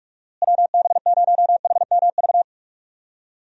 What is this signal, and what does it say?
Morse code, heard as a keyed tone, about 43 wpm, 700 Hz. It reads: WB0HMV
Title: USA Call Signs - 6 Characters - Repeated 3x - 45wpm